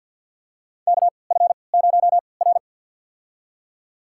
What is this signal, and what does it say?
Morse code, heard as a keyed tone, about 53 wpm, spaced as for 29 wpm, 700 Hz. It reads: KF0R
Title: USA Call Signs - 4 Characters - 3x Character Spacing - 50wpm